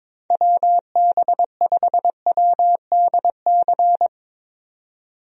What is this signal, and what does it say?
Morse code, heard as a keyed tone, about 22 wpm, 700 Hz. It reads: WB5WDC
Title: USA Call Signs - 6 Characters - 22wpm